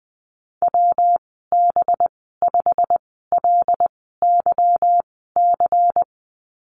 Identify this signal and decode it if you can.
Morse code, heard as a keyed tone, about 20 wpm, spaced as for 15 wpm, 700 Hz. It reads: WB5LYC
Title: 193 Lesson 31g - Letter v - Call Signs 20wpm